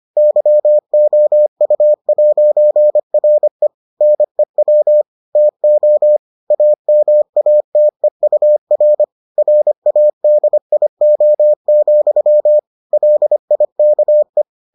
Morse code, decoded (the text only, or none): YOU'RE NEW TO AMATEUR RADIO, LIKE